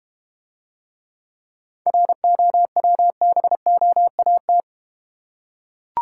Morse code, heard as a keyed tone, about 32 wpm, 700 Hz. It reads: ROWBOAT